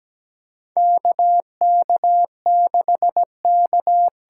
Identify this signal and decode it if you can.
Morse code, heard as a keyed tone, about 17 wpm, 700 Hz. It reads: KK6K
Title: USA Call Signs - 6 Characters - 8x Character Spacing - 17wpm